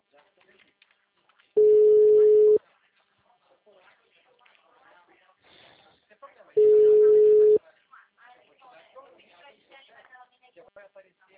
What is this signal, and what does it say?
Ringback tone (Europe, 425 Hz), with speech.